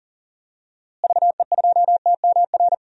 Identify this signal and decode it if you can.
Morse code, heard as a keyed tone, about 40 wpm, 700 Hz. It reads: VE2TMR
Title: DXCC-100 Call Signs - 6 Characters - Repeated 3x - 40wpm